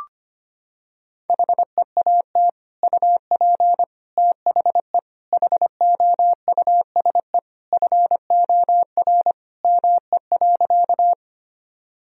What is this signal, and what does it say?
Morse code, heard as a keyed tone, about 25 wpm, 700 Hz. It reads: HEAT UP THE HOUSE FOR ME.
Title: Sentences from Top 1000 Words - Review Part 3 - 25wpm